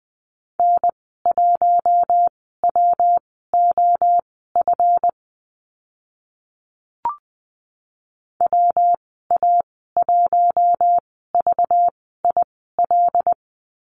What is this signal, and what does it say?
Morse code, heard as a keyed tone, about 20 wpm, spaced as for 15 wpm, 700 Hz. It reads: N1WOF WA1VIL
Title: 193 Lesson 31g - Letter v - Call Signs 20wpm